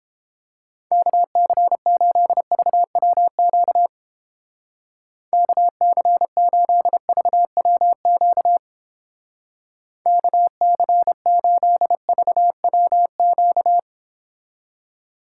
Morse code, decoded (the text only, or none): KC8VWQ KC8VWQ KC8VWQ